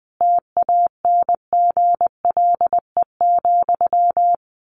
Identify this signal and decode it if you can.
Morse code, heard as a keyed tone, about 20 wpm, 700 Hz. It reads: TANGLE,
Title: A Sound Of Thunder 20wpm